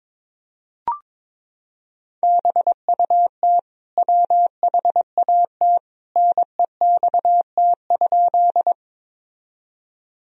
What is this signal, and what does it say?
Morse code, heard as a keyed tone, about 22 wpm, 700 Hz. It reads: BUT WHAT NEXT?
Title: Sentences from Top 1000 Words - Review Part 1 - 22wpm